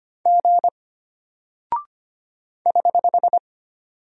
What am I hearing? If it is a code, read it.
Morse code, heard as a keyed tone, about 25 wpm, 700 Hz. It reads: G <HH>
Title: Single Character (Letter:Number:Punctuation:Prosign) Alt 25wpm